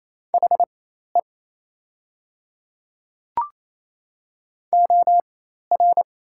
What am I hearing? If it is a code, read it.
Morse code, heard as a keyed tone, about 27 wpm, spaced as for 13 wpm, 690 Hz. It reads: HE OR
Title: 4x Character Spacing - Top 100 Words - 28wpm